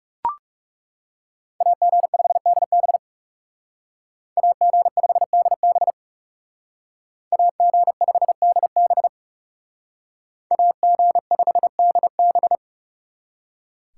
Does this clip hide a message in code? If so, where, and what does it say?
Yes: Morse code from 1.6 to 12.6 s, heard as a keyed tone, about 38 wpm, 700 Hz. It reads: AG5DB AG5DB AG5DB AG5DB